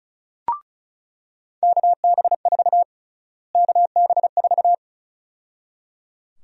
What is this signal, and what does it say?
Morse code, heard as a keyed tone, about 34 wpm, 700 Hz. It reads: KB4 KB4
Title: USA Call Sign Prefixes - Repeated 2x - 35wpm